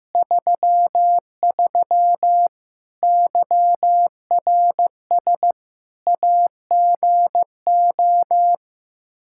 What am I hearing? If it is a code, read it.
Morse code, heard as a keyed tone, about 15 wpm, 700 Hz. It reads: 33 YRS AGO
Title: Practice QSO - Exchange 3 - More Info - 15wpm